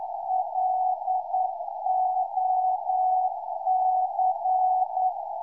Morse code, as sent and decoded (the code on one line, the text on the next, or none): .-.. --- -.-.
LOC